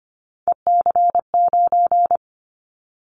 Morse code, heard as a keyed tone, about 25 wpm, 700 Hz. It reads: EC9
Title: Callsign Prefixes for Top 100 DXCC - 25wpm